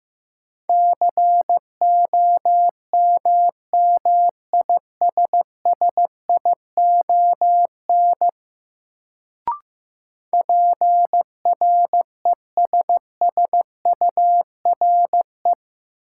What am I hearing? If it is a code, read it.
Morse code, heard as a keyed tone, about 15 wpm, 700 Hz. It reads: COMMISSION PRESSURE
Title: Sentences from Top 1400 Words - New Word Drill Alt - 15wpm